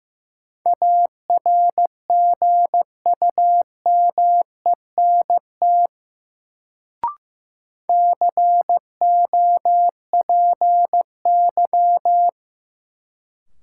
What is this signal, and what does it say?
Morse code, heard as a keyed tone, about 15 wpm, 700 Hz. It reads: ARGUMENT COPY
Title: Sentences from Top 1400 Words - New Word Drill Alt - 15wpm